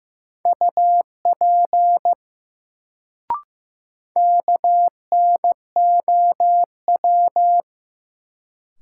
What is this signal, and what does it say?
Morse code, heard as a keyed tone, about 16 wpm, 700 Hz. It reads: UP KNOW
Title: Sentences from Top 100 Words - New Word Drill - 15wpm